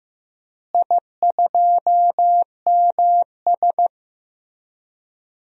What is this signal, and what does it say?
Morse code, heard as a keyed tone, about 15 wpm, 700 Hz. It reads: I2MS
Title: DXCC-100 Call Signs - 4 Characters - Repeated 3x - 15wpm